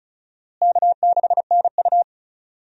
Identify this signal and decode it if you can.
Morse code, heard as a keyed tone, about 36 wpm, 700 Hz. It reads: K6NU
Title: USA Call Signs - 4 Characters - Repeated 2x - 35wpm